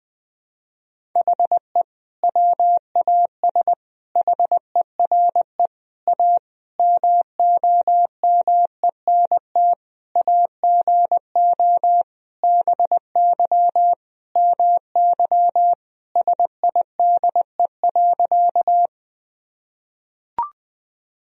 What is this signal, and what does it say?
Morse code, heard as a keyed tone, about 20 wpm, 700 Hz. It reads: HE WAS HERE A MOMENT AGO BY MY SIDE.
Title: Sentences from Top 1000 Words - Review Part 3 - 20wpm